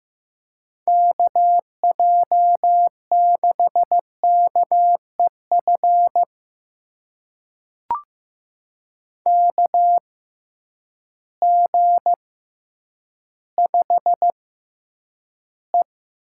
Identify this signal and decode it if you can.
Morse code, heard as a keyed tone, about 15 wpm, spaced as for 9 wpm, 700 Hz. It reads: KJ6KEF KG5E
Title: USA Call Signs - 6 Characters - 6x Character Spacing - 15wpm